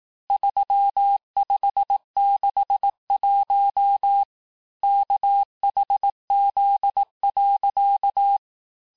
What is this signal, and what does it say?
Morse code, heard as a keyed tone, about 18 wpm, 800 Hz. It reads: 3561 KHZ.
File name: cours-du-lundi files / cours-1594--teste-chiffre-vitesse-17mpm